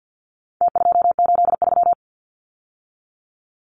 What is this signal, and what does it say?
Morse code, heard as a keyed tone, about 50 wpm, 700 Hz. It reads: T283
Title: Numbers - Contest Style - 4-Digit 50wpm